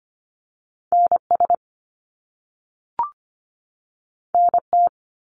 Morse code, heard as a keyed tone, about 25 wpm, 700 Hz. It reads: NS NT
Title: Canadian Provinces and Territories 25wpm